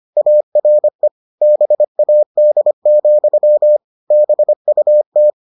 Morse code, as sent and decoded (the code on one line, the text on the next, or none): .- .-. / -... .- -.. --..-- / -... ..- -
ARE BAD, BUT